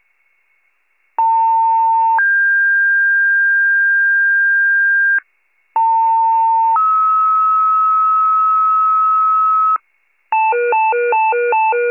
Two-tone page, heard as a one-second tone, then a three-second tone, about 900 then 1600 Hz, about 900 then 1290 Hz, with noise. It is heard from 1.2 to 9.8 s.
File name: wcalert